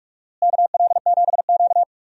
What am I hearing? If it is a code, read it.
Morse code, heard as a keyed tone, about 42 wpm, 700 Hz. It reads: KL7Q
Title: DXCC-100 Call Signs - 4 Characters - Repeated 3x - 45wpm